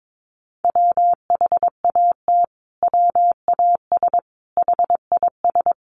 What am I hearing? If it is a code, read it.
Morse code, heard as a keyed tone, about 22 wpm, 700 Hz. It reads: WHAT WAS HIS
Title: Sentences from Top 1400 Words 22wpm